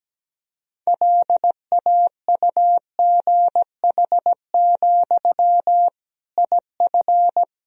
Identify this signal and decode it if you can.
Morse code, heard as a keyed tone, about 17 wpm, 700 Hz. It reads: LAUGH, IF
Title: Sentences from Top 500 Words and Abbreviations 17wpm